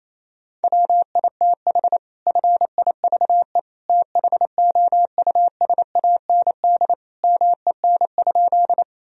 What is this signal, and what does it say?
Morse code, heard as a keyed tone, about 28 wpm, 700 Hz. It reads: WITH FIVE THOUSAND MEN?
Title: Sentences from Top 1000 Words - Review Part 3 - 28wpm